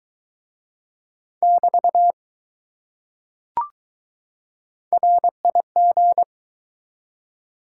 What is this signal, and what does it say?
Morse code, heard as a keyed tone, about 20 wpm, 700 Hz. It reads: <BT> RIG